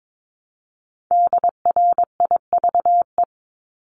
Morse code, heard as a keyed tone, about 22 wpm, 700 Hz. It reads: DRIVE
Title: Multi-letter Phonemes Repeated 3x 22wpm